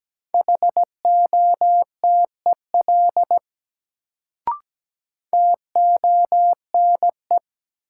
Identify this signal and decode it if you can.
Morse code, heard as a keyed tone, about 17 wpm, 700 Hz. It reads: HOTEL TONE
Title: Sentences from Top 1600 Words - New Word Drill Alt - 17wpm